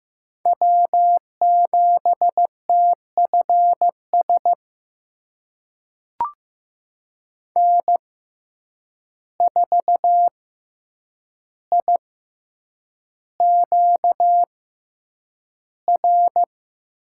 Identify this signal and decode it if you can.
Morse code, heard as a keyed tone, about 15 wpm, spaced as for 8 wpm, 700 Hz. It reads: W7TFS N4IQR
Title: USA Call Signs - 6x Character Spacing - 15wpm